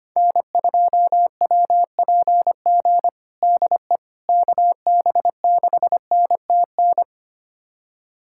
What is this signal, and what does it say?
Morse code, heard as a keyed tone, about 25 wpm, 700 Hz. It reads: N2WPG DE KB6NTN
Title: USA Callsign DE Callsign 25wpm